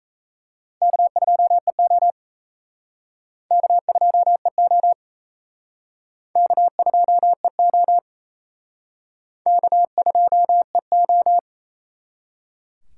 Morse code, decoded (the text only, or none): K2EO K2EO K2EO K2EO